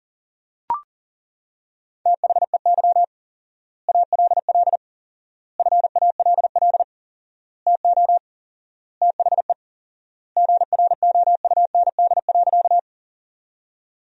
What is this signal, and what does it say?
Morse code, heard as a keyed tone, about 40 wpm, 700 Hz. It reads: THEY ALL FALL TO THE GROUND.
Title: 4x Word-Spacing - Sentences from Top 500 Words - 40wpm